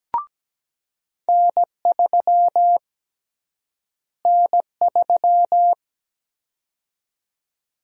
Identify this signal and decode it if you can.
Morse code, heard as a keyed tone, about 17 wpm, 700 Hz. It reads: N3 N3
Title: USA Call Sign Prefixes - Repeated 2x - 17wpm